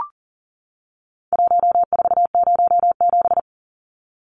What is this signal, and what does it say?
Morse code, heard as a keyed tone, about 40 wpm, 700 Hz. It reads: 1407
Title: Numbers - Contest Style - 4-Digit 40wpm